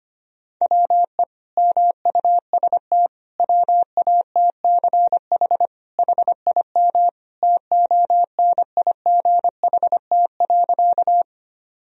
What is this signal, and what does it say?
Morse code, heard as a keyed tone, about 25 wpm, 700 Hz. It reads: WE MUST WATCH HIM TONIGHT.
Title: Sentences from Top 1000 Words - Review Part 3 - 25wpm